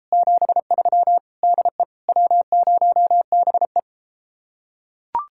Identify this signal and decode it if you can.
Morse code, heard as a keyed tone, about 32 wpm, 700 Hz. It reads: 73 DE W0BE